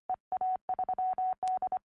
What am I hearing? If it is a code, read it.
Morse code, heard as a keyed tone, about 24 wpm, 750 Hz. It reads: EA3D
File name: ea3d 4m